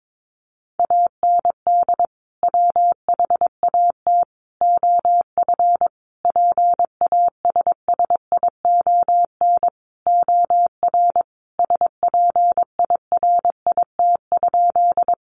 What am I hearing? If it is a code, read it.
Morse code, heard as a keyed tone, about 22 wpm, 700 Hz. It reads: AND WHAT OF PASSION OR SPIRIT?
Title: Sentences from Top 1400 Words 22wpm